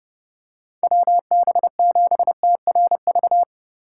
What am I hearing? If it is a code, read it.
Morse code, heard as a keyed tone, about 30 wpm, 700 Hz. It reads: WB7TRV
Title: USA Call Signs - 6 Characters - 2x Character Spacing - 30wpm